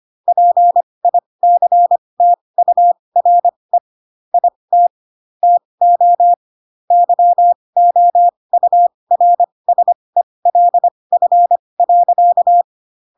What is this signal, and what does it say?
Morse code, heard as a keyed tone, about 25 wpm, spaced as for 20 wpm, 700 Hz. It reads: PICTURE IT TO YOURSELF.